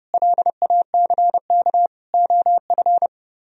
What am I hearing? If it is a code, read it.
Morse code, heard as a keyed tone, about 29 wpm, 700 Hz. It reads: LACK OF